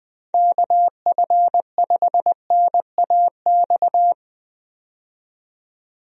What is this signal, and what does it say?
Morse code, heard as a keyed tone, about 20 wpm, 700 Hz. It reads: KF5NAX
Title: USA Call Signs - 6 Characters - 20wpm